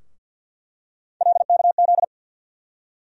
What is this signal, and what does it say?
Morse code, heard as a keyed tone, about 49 wpm, 700 Hz. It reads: LKZ
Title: USA Call Sign Postfixes - 3 Letters 50wpm